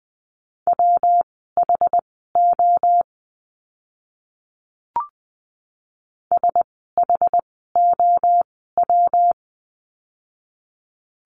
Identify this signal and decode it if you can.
Morse code, heard as a keyed tone, about 20 wpm, spaced as for 14 wpm, 700 Hz. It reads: WHO SHOW
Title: 230 Lesson 37f - Letter x - Words 20wpm